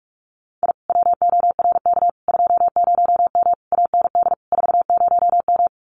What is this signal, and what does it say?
Morse code, heard as a keyed tone, about 46 wpm, 700 Hz. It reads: I WORK 20M AND 40M
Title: Common QSO Phrases Encore 45wpm